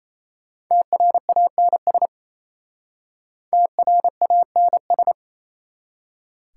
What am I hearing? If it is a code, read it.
Morse code, heard as a keyed tone, about 37 wpm, 700 Hz. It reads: TRANS TRANS